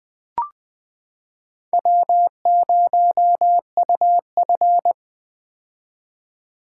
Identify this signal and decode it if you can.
Morse code, heard as a keyed tone, about 21 wpm, 700 Hz. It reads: W0UF
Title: K1USN SST QSO Practice - 20wpm